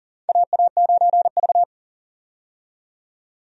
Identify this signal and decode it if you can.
Morse code, heard as a keyed tone, about 39 wpm, 700 Hz. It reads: AA9V
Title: USA Call Signs - 4 Characters - 40wpm